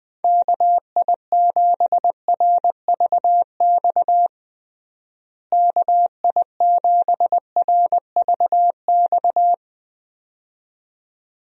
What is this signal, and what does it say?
Morse code, heard as a keyed tone, about 20 wpm, 700 Hz. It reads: KI7RVX KI7RVX
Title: USA Call Signs - 6 Characters - Repeated 2x - 20wpm